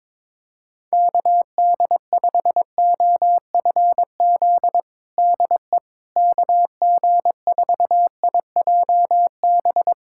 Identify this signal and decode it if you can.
Morse code, heard as a keyed tone, about 22 wpm, 700 Hz. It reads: KD5OFZ DE KG4IJB
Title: USA Callsign DE Callsign - 6 Characters - 22wpm